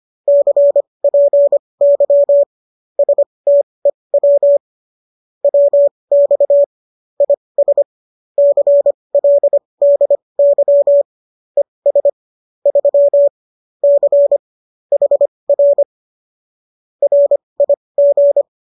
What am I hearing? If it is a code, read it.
Morse code, heard as a keyed tone, about 25 wpm, spaced as for 20 wpm, 570 Hz. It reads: CPY STEW WX IS CLDY ES 3 C HR RIG